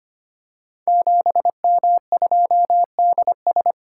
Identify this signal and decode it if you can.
Morse code, heard as a keyed tone, about 25 wpm, 700 Hz. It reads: 7M2DS
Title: Code Groups - 5 Letter:Numbers - 25wpm